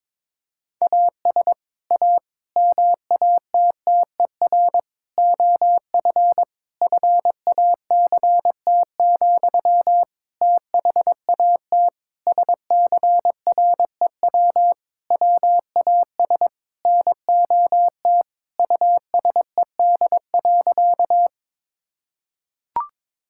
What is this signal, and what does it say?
Morse code, heard as a keyed tone, about 22 wpm, 700 Hz. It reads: AS A MATTER OF FACT, THAT SCREW WAS NOT USED.
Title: Sentences from Top 1600 Words 22wpm